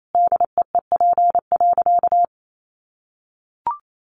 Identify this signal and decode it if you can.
Morse code, heard as a keyed tone, about 28 wpm, 700 Hz. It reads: DEEP.
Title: Sentences from Top 1100 Words 28wpm